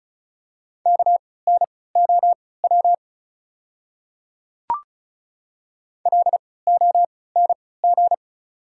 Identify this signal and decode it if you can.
Morse code, heard as a keyed tone, about 35 wpm, spaced as for 20 wpm, 690 Hz. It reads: KNOW LONG
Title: 3x Character Spacing - Top 100 Words - 35wpm